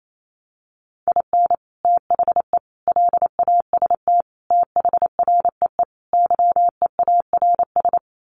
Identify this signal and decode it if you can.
Morse code, heard as a keyed tone, about 28 wpm, 700 Hz. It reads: IN THE LAST THREE YEARS
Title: Sets of 5 Words Encore2 28wpm